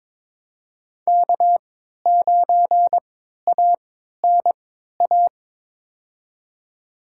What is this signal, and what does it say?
Morse code, heard as a keyed tone, about 22 wpm, spaced as for 13 wpm, 700 Hz. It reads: K9ANA
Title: USA Call Signs - 5 Characters - 3x Character Spacing - 22wpm